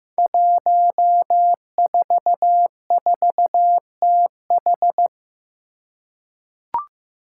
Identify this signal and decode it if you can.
Morse code, heard as a keyed tone, about 15 wpm, 700 Hz. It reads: 144TH
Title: Five Challenging Letter-Number Combinations 15wpm